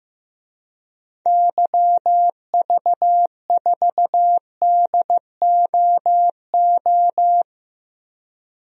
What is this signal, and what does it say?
Morse code, heard as a keyed tone, about 16 wpm, 700 Hz. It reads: YV4DOO
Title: DXCC-100 Call Signs - 6 Characters - Repeated 3x - 15wpm